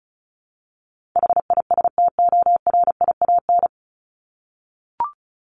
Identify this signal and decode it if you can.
Morse code, heard as a keyed tone, about 34 wpm, 700 Hz. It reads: HISTORIAN
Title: Top 1751-2000 Words 35wpm